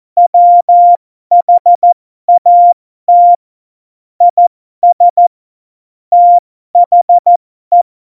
Morse code, heard as a keyed tone, about 14 wpm, spaced as for 12 wpm, 700 Hz. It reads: WHAT IS THE
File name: Morsekurs / Witze